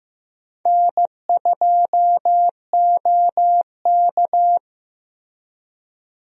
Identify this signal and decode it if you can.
Morse code, heard as a keyed tone, about 15 wpm, 700 Hz. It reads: N2OK
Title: USA Call Signs - 4 Characters - Repeated 3x - 15wpm